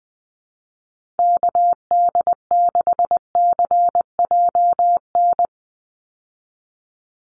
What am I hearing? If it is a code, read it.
Morse code, heard as a keyed tone, about 20 wpm, 700 Hz. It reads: KD6CJN
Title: USA Call Signs 20wpm